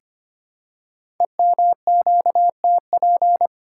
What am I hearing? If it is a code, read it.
Morse code, heard as a keyed tone, about 25 wpm, 700 Hz. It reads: EMQTP
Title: Code Groups - 5 Letters - 25wpm